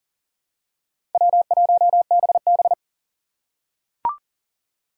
Morse code, heard as a keyed tone, about 40 wpm, 700 Hz. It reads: W1BB
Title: USA Call Signs - 4 Characters - 6x Character Spacing - 40wpm